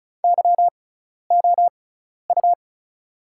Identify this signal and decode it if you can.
Morse code, heard as a keyed tone, about 36 wpm, spaced as for 12 wpm, 700 Hz. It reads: YOU
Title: 6x Character Spacing - Top 100 Words - 35wpm